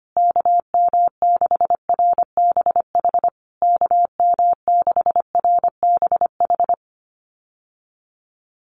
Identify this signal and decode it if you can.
Morse code, heard as a keyed tone, about 25 wpm, 700 Hz. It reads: KM6RBH KM6RBH
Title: USA Call Signs Repeated 3x 25wpm